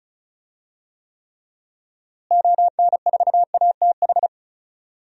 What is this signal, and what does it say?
Morse code, heard as a keyed tone, about 36 wpm, 700 Hz. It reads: ON4ATH
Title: DXCC-100 Call Signs - 6 Characters - Repeated 2x - 35wpm